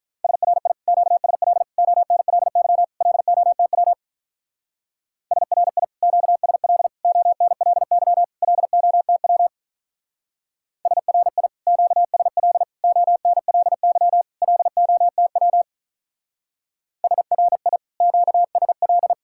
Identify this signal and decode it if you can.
Morse code, heard as a keyed tone, about 43 wpm, 700 Hz. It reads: SRI QSL ONLY LOTW SRI QSL ONLY LOTW SRI QSL ONLY LOTW SRI QSL
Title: Speed-Racing Common QSO Phrases - 35wpm